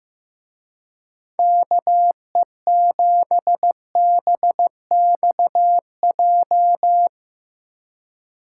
Morse code, heard as a keyed tone, about 16 wpm, 700 Hz. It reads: KE7BXJ
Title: Speed Racing - USA Call Signs 15wpm